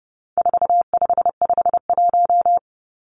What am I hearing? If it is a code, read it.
Morse code, heard as a keyed tone, about 30 wpm, 700 Hz. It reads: 4551
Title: Numbers - 4-Digit 30wpm